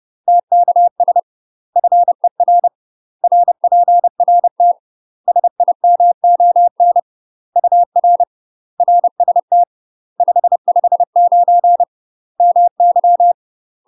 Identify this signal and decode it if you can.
Morse code, heard as a keyed tone, about 30 wpm, 700 Hz. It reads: TKS FER RPRT SIMON UR RST 559 MY
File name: UK MASTER QSOs TRAINING / 30WPM / UK MASTER QSOs 0004